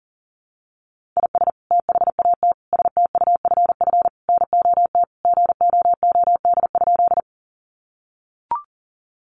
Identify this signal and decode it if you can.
Morse code, heard as a keyed tone, about 40 wpm, 700 Hz. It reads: IS THAT STUFF NOT GOOD?
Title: Sentences from Top 1400 Words 40wpm